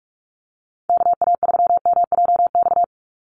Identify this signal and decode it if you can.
Morse code, heard as a keyed tone, about 44 wpm, 700 Hz. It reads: KA3MJX
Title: USA Call Signs 45wpm